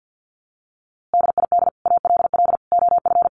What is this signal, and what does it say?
Morse code, heard as a keyed tone, about 50 wpm, 700 Hz. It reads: DID ALL OF
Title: A Sound Of Thunder 50wpm